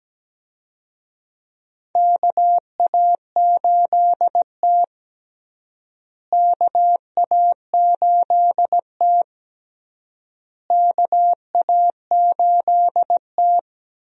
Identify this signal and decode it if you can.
Morse code, heard as a keyed tone, about 17 wpm, 700 Hz. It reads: KA8T KA8T KA8T
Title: USA Call Signs - 4 Characters - Repeated 3x - 17wpm